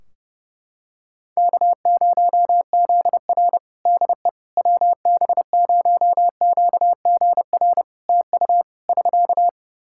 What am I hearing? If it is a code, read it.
Morse code, heard as a keyed tone, about 32 wpm, 700 Hz. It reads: K0ZR DE WB0QGR TU <SK>
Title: Practice QSO - Exchange 4 - Ending - 30wpm